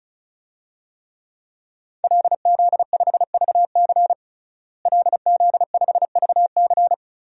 Morse code, heard as a keyed tone, about 36 wpm, 700 Hz. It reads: LZ5VC LZ5VC
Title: DXCC-100 Call Signs - 5 Characters - Repeated 3x - 35wpm